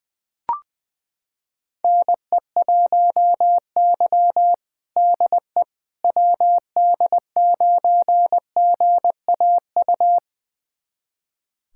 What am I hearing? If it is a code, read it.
Morse code, heard as a keyed tone, about 20 wpm, 700 Hz. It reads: NE1Y DE WD9GAU
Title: USA Callsign DE Callsign 20wpm